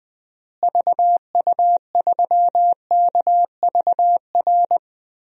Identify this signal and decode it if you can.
Morse code, heard as a keyed tone, about 20 wpm, 700 Hz. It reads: VU3KVR
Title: DXCC-100 Call Signs - 6 Characters - Repeated 3x - 20wpm